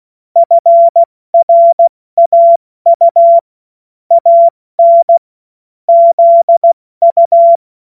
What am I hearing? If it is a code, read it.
Morse code, heard as a keyed tone, about 16 wpm, 680 Hz. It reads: FRAU AN ZU
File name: Morsekurs / Witze / Witz 05 Dezember 2023 70 F80